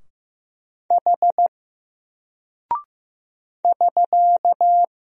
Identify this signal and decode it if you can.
Morse code, heard as a keyed tone, about 15 wpm, 700 Hz. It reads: H <SK>
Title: Single Character (Letter:Number:Punctuation:Prosign) Alt 15wpm